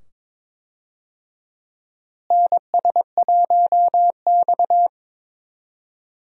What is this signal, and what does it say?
Morse code, heard as a keyed tone, about 21 wpm, 700 Hz. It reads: NS1X
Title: USA Call Signs - 4 Characters - Repeated 2x - 22wpm